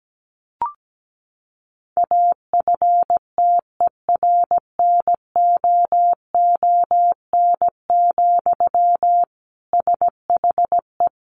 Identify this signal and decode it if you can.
Morse code, heard as a keyed tone, about 17 wpm, 700 Hz. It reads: AFTERNOON, SHE
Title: Sentences from Top 1500 Words 17wpm